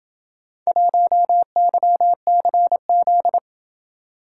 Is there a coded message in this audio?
Yes: Morse code from 0.7 to 3.4 s, heard as a keyed tone, about 28 wpm, 700 Hz. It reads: 1YCZ